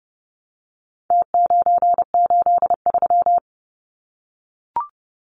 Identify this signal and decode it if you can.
Morse code, heard as a keyed tone, about 30 wpm, 700 Hz. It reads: T983
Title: Numbers - Contest Style - 4-Digit 30wpm